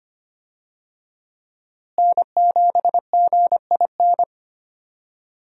Morse code, heard as a keyed tone, about 25 wpm, 700 Hz. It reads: N7GIN
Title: USA Call Signs - 5 Characters - Repeated 3x - 25wpm